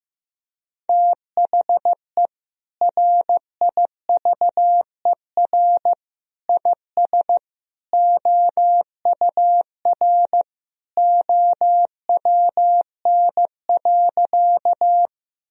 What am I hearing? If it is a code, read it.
Morse code, heard as a keyed tone, about 15 wpm, 700 Hz. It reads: THE RIVER IS OUR OWN.
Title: Sentences from Top 1000 Words - Review Part 1 - 15wpm